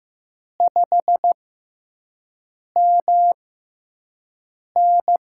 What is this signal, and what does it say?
Morse code, heard as a keyed tone, about 15 wpm, spaced as for 5 wpm, 700 Hz. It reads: 5MN
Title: USA Call Signs - 4 Characters - 6x Character Spacing - 15wpm